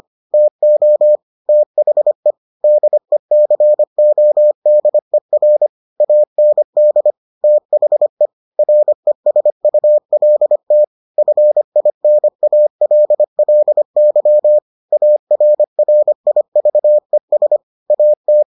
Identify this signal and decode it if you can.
Morse code, heard as a keyed tone, about 25 wpm, 600 Hz. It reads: TO THE DECODER AND THE RESULT FINALLY ARRIVES AT